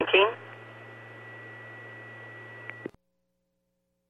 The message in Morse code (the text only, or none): none